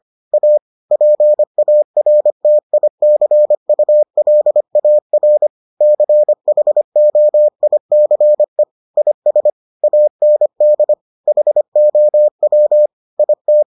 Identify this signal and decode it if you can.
Morse code, heard as a keyed tone, about 25 wpm, 600 Hz. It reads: A PARTICULAR CHOICE IS AND HOW IT